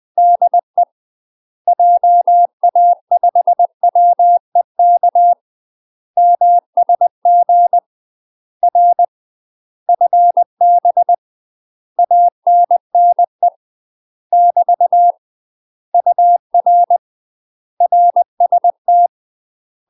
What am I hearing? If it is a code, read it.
Morse code, heard as a keyed tone, about 20 wpm, 700 Hz. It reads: DE JA5WEK MSG R FB ANNE <BT> UR RST
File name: UK MASTER QSOs TRAINING / 20WPM / UK MASTER QSOs 0023